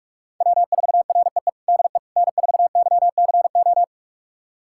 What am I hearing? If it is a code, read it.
Morse code, heard as a keyed tone, about 46 wpm, 700 Hz. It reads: W4REE DE N4YCO
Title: USA Callsign DE Callsign - 5 Characters - 45wpm